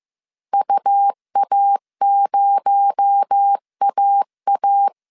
Morse code, heard as a keyed tone, about 15 wpm, 800 Hz. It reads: UA0AA
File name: CW / DET / 1 9 1500C S C 14wpm 3245